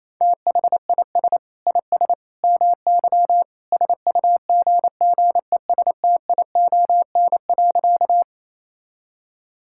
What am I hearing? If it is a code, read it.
Morse code, heard as a keyed tone, about 28 wpm, 700 Hz. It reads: THIS IS MY SUGGESTION.
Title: Sentences from Top 1600 Words 28wpm